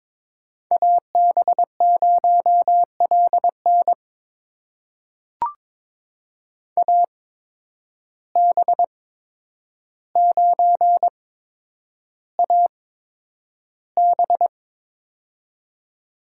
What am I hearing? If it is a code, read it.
Morse code, heard as a keyed tone, about 22 wpm, spaced as for 9 wpm, 700 Hz. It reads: AB0LN AB9AB
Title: USA Call Signs - 5 Characters - 8x Character Spacing - 22wpm